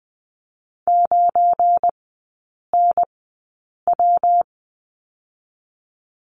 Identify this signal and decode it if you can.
Morse code, heard as a keyed tone, about 20 wpm, spaced as for 8 wpm, 690 Hz. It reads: 9NW
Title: 191 Lesson 31e - Letter v - Three Characters 20wpm